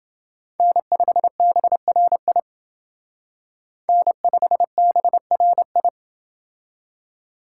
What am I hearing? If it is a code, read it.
Morse code, heard as a keyed tone, about 28 wpm, 700 Hz. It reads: N5BRI N5BRI